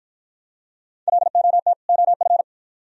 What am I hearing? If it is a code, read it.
Morse code, heard as a keyed tone, about 53 wpm, 700 Hz. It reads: LOT OF